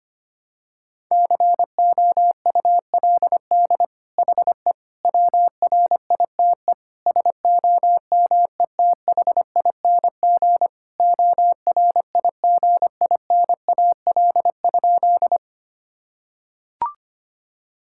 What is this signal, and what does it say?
Morse code, heard as a keyed tone, about 25 wpm, 700 Hz. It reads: COULD HE WRITE SOMETHING ORIGINAL?
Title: Sentences from Top 1000 Words - Review Part 6 - 25wpm